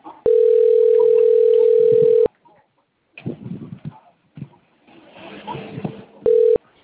Ringback tone (North America), 440 + 480 Hz.